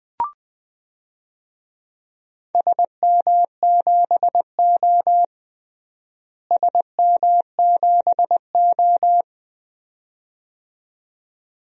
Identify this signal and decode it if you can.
Morse code, heard as a keyed tone, about 20 wpm, 700 Hz. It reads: SM7O SM7O
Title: DXCC-100 Call Signs - 4 Characters - Repeated 2x - 20wpm